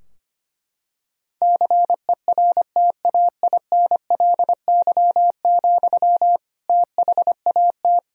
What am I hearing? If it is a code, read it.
Morse code, heard as a keyed tone, about 25 wpm, 700 Hz. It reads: CERTAINLY, THAT
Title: Sentences from Top 1000 Words - Review Part 2 - 25wpm